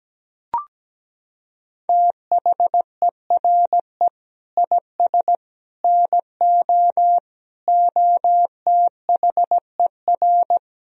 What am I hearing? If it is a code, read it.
Morse code, heard as a keyed tone, about 17 wpm, 700 Hz. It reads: THERE IS NO OTHER
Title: Sentences from Top 1000 Words - Review Part 4 - 17wpm